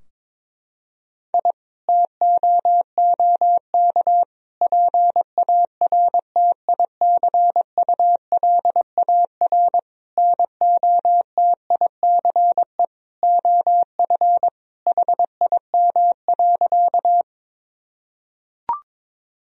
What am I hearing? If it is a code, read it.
Morse code, heard as a keyed tone, about 22 wpm, 700 Hz. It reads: I TOOK PARTICULAR NOTICE OF HIM.
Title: Sentences from Top 1000 Words - Review Part 4 - 22wpm